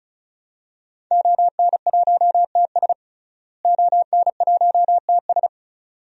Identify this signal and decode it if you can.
Morse code, heard as a keyed tone, about 36 wpm, 700 Hz. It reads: ON1TS ON1TS
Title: DXCC-100 Call Signs - 5 Characters - Repeated 2x - 35wpm